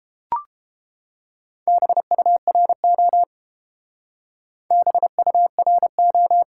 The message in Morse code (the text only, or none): BURO BURO